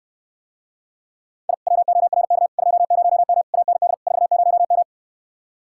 Morse code, heard as a keyed tone, about 69 wpm, 700 Hz. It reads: I WORK 20M AND 40M